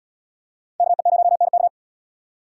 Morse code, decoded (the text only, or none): KE0NK